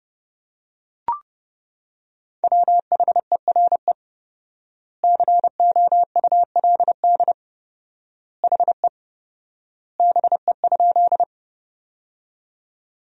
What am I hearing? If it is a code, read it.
Morse code, heard as a keyed tone, about 30 wpm, 700 Hz. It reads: WHERE COULD HE BE?
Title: 4x Word-Spacing - Sentences from Top 200 Words - 30wpm